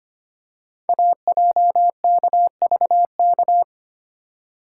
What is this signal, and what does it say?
Morse code, heard as a keyed tone, about 25 wpm, 700 Hz. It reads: AJKVK
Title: Code Groups - 5 Letters - 25wpm